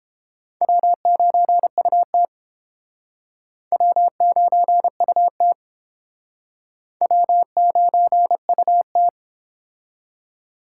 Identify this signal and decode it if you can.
Morse code, heard as a keyed tone, about 29 wpm, 700 Hz. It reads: W9UT W9UT W9UT